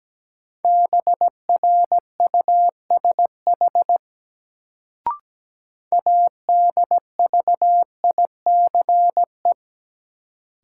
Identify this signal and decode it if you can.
Morse code, heard as a keyed tone, about 17 wpm, 700 Hz. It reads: BRUSH ADVICE
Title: Sentences from Top 1100 Words - New Word Drill Alt - 17wpm